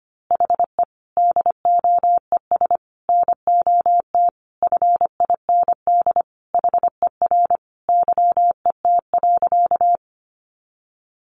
Morse code, heard as a keyed tone, about 25 wpm, 700 Hz. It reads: HE DOES NOT FIND HER YET.
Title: Easy Phrases 25wpm